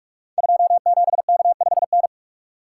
Morse code, heard as a keyed tone, about 46 wpm, 700 Hz. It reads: 27K5N
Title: Code Groups - 5 Letter:Numbers - 45wpm